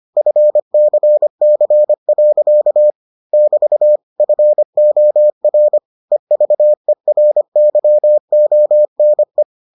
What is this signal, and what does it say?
Morse code, heard as a keyed tone, about 25 wpm, 600 Hz. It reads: FCC. <BT> FOR EVERYONE